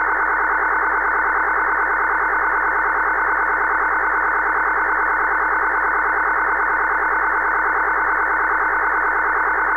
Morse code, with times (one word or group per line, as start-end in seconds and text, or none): none